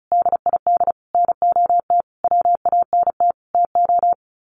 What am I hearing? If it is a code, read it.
Morse code, heard as a keyed tone, about 34 wpm, 700 Hz. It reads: DID NOT WANT TO
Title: Sets of 5 Words 35wpm